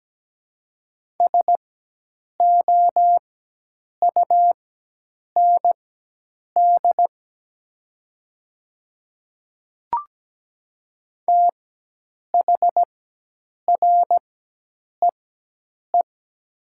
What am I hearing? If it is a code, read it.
Morse code, heard as a keyed tone, about 17 wpm, spaced as for 8 wpm, 700 Hz. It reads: SOUND THREE
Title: 4x Character Spacing - Top 200 Words - 17wpm